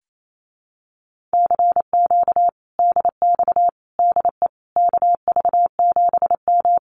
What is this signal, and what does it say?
Morse code, heard as a keyed tone, about 28 wpm, 700 Hz. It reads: CQ DX DE KV7M
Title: Common QSO Phrases Encore 28wpm